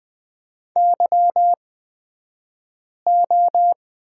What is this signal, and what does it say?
Morse code, heard as a keyed tone, about 20 wpm, spaced as for 5 wpm, 700 Hz. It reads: YO
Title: 254 Lesson 41d - Period - Two Characters 20wpm